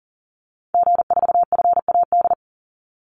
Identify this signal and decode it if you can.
Morse code, heard as a keyed tone, about 40 wpm, 700 Hz. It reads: G4FAD
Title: Call Signs Repeated 3x - 40wpm